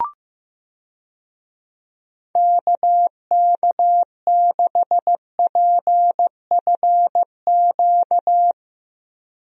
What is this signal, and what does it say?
Morse code, heard as a keyed tone, about 15 wpm, 700 Hz. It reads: KK6PFQ
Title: USA Call Signs - 6 Characters - Repeated 3x - 15wpm